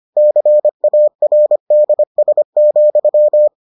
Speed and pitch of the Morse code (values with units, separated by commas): 25 wpm, 600 Hz